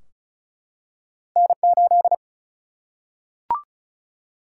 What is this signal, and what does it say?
Morse code, heard as a keyed tone, about 36 wpm, 700 Hz. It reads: N 8
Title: Single-Letter Single-Number 35wpm